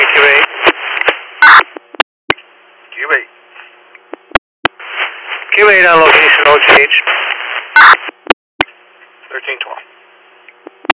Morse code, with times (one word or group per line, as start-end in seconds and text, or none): none